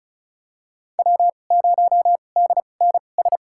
Morse code, heard as a keyed tone, about 36 wpm, spaced as for 25 wpm, 700 Hz. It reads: W0DNS
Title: USA Call Signs - 5 Characters - 2x Character Spacing - 35wpm